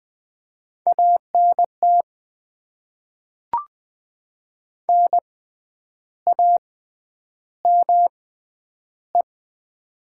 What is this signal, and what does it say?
Morse code, heard as a keyed tone, about 20 wpm, spaced as for 9 wpm, 700 Hz. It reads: ANT NAME
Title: Top 20 QSO Elements - 6x Character Spacing - 20wpm